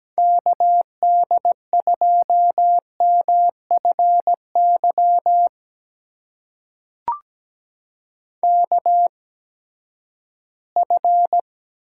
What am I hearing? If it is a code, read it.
Morse code, heard as a keyed tone, about 17 wpm, spaced as for 12 wpm, 700 Hz. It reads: KD2MFY KF
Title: USA Call Signs - 6 Characters - 8x Character Spacing - 17wpm